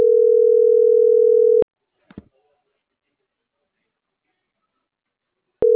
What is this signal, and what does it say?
Ringback tone (North America), 440 + 480 Hz.